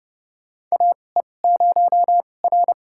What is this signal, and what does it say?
Morse code, heard as a keyed tone, about 30 wpm, spaced as for 22 wpm, 700 Hz. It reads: AE0R
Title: USA Call Signs - 4 Characters - 2x Character Spacing - 30wpm